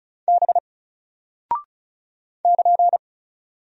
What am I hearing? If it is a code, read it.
Morse code, heard as a keyed tone, about 36 wpm, 700 Hz. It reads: B <KN>
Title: Single Character (Letter:Number:Punctuation:Prosign) 35wpm